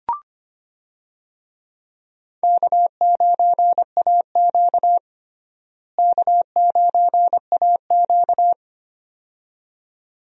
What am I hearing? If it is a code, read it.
Morse code, heard as a keyed tone, about 25 wpm, 700 Hz. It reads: K9AQ K9AQ
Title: USA Call Signs - 4 Characters - Repeated 2x - 25wpm